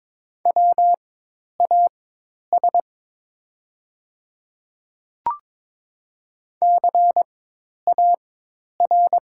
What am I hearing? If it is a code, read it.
Morse code, heard as a keyed tone, about 22 wpm, spaced as for 10 wpm, 700 Hz. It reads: WAS CAR
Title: 4x Character Spacing - Top 300 Words - 22wpm